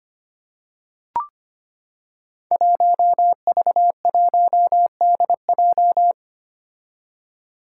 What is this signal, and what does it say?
Morse code, heard as a keyed tone, about 25 wpm, 700 Hz. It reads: 1V1DJ
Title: Five Challenging Letter-Number Combinations 25wpm